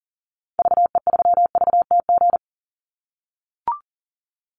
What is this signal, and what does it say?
Morse code, heard as a keyed tone, about 40 wpm, 700 Hz. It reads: VE3VTG
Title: Call Signs - 40wpm